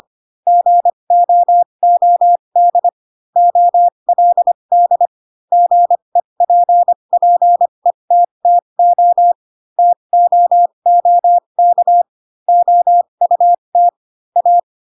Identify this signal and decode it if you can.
Morse code, heard as a keyed tone, about 25 wpm, spaced as for 22 wpm, 700 Hz. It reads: GOOD OLD GEPPETTO TOOK OUT A
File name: Pinocchio / Pinocchio ch7 18 25 ik1hge